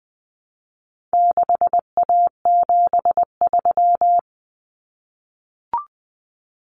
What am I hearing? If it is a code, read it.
Morse code, heard as a keyed tone, about 20 wpm, 700 Hz. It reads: 6A73
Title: Common Cut Numbers - 4-Digit 20wpm